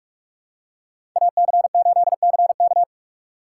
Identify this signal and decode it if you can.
Morse code, heard as a keyed tone, about 46 wpm, 700 Hz. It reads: AC8CK